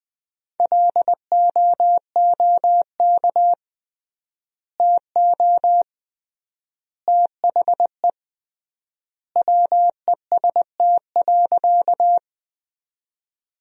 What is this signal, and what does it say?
Morse code, heard as a keyed tone, about 21 wpm, 700 Hz. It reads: LOOK TO THE WEST.
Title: 3x Word-Spacing - Sentences from Top 400 Words - 20wpm